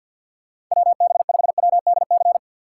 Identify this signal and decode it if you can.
Morse code, heard as a keyed tone, about 53 wpm, 700 Hz. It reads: WB5WDC
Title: USA Call Signs - 6 Characters - 4x Character Spacing - 50wpm